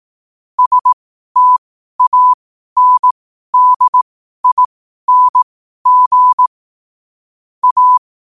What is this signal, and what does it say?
Morse code, heard as a keyed tone, about 18 wpm, spaced as for 12 wpm, 1000 Hz. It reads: STANDING A
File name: qotdfiles / 12wpm